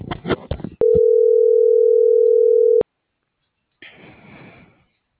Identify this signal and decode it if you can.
Ringback tone (North America), 440 + 480 Hz.